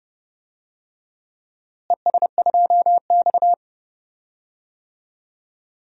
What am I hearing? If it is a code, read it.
Morse code, heard as a keyed tone, about 29 wpm, 700 Hz. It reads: ES2X
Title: DXCC-100 Call Signs - 4 Characters - Repeated 2x - 30wpm